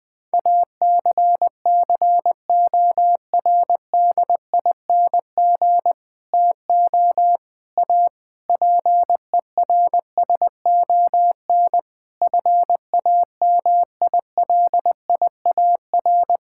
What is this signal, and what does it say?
Morse code, heard as a keyed tone, about 20 wpm, 700 Hz. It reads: ACCORDING TO A PERSON FAMILIAR
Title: Sentences with 7-Word Phrases - Spoken Beginning - 20wpm